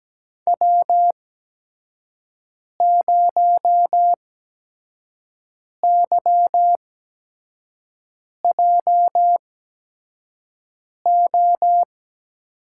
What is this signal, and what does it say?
Morse code, heard as a keyed tone, about 17 wpm, spaced as for 5 wpm, 700 Hz. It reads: W0YJO
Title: USA Call Signs - 8x Character Spacing - 17wpm